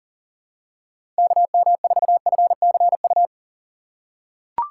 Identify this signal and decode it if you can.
Morse code, heard as a keyed tone, about 39 wpm, 700 Hz. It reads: KM4FCU
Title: USA Call Signs - 6 Characters - 6x Character Spacing - 40wpm